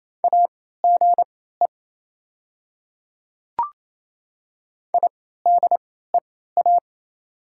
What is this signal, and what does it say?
Morse code, heard as a keyed tone, about 27 wpm, spaced as for 16 wpm, 700 Hz. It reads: AGE IDEA
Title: 3x Character Spacing - Top 500 Words - 28wpm